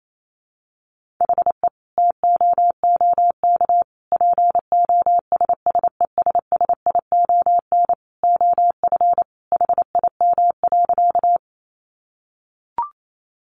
Morse code, heard as a keyed tone, about 28 wpm, 700 Hz. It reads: HE TOOK POSSESSION OF HIM.
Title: Sentences from Top 1400 Words 28wpm